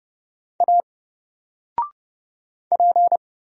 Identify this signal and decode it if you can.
Morse code, heard as a keyed tone, about 30 wpm, 700 Hz. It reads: A P
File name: Single-Letter-Number-30wpm